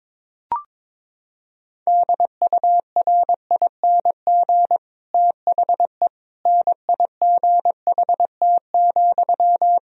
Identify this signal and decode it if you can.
Morse code, heard as a keyed tone, about 22 wpm, 700 Hz. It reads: DURING THE NIGHT,
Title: Sentences from Top 1000 Words - Review Part 2 - 22wpm